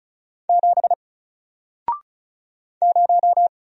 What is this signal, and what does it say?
Morse code, heard as a keyed tone, about 35 wpm, 700 Hz. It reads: 7 0
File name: Single-Letter-Number-35wpm